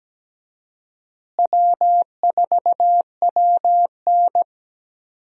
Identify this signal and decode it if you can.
Morse code, heard as a keyed tone, about 17 wpm, 700 Hz. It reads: W4WN
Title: USA Call Signs - 4 Characters - Repeated 2x - 17wpm